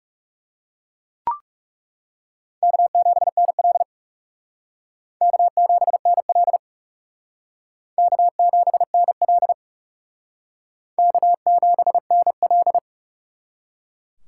Morse code, heard as a keyed tone, about 37 wpm, 700 Hz. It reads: K7NL K7NL K7NL K7NL
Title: Speed Racing - USA 1x2 Call Signs 30wpm